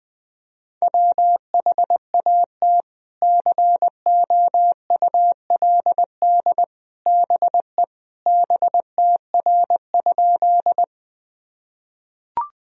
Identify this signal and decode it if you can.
Morse code, heard as a keyed tone, about 20 wpm, 700 Hz. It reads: WHAT COULD BE BTR?
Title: Sentences from Top 400 Words and Abbreviations 20wpm